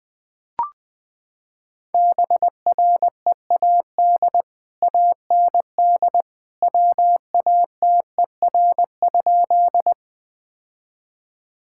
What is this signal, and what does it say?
Morse code, heard as a keyed tone, about 20 wpm, 700 Hz. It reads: BREAD AND WATER?
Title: Sentences from Top 1600 Words 20wpm